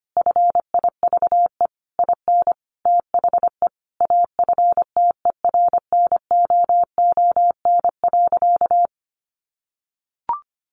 Morse code, heard as a keyed tone, about 25 wpm, 700 Hz. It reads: FIVE IN THE AFTERNOON.
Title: Sentences from Top 1500 Words 25wpm